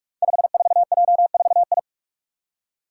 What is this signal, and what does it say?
Morse code, heard as a keyed tone, about 44 wpm, 700 Hz. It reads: 54J4I
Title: Five Challenging Letter-Number Combinations 45wpm